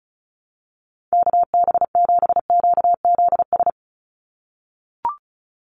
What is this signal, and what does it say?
Morse code, heard as a keyed tone, about 34 wpm, 700 Hz. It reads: KB7QZS
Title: USA Call Signs 35wpm